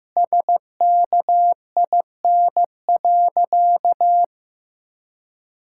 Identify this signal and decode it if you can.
Morse code, heard as a keyed tone, about 16 wpm, 700 Hz. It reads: SKIN.
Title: Sentences from Top 1600 Words 15wpm